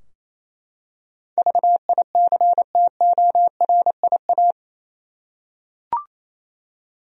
Morse code, heard as a keyed tone, about 28 wpm, 700 Hz. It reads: VICTORIA
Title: USA Common Names - 28wpm